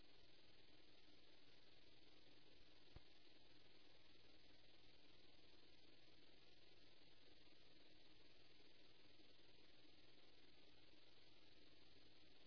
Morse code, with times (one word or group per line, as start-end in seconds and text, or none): none